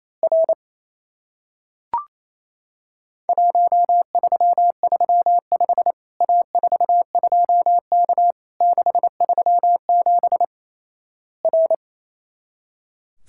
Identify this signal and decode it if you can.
Morse code, heard as a keyed tone, about 28 wpm, 700 Hz. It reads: R 1335 A42K 637 R
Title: Worked All Europe DX Contest - QTC Exchange - 28wpm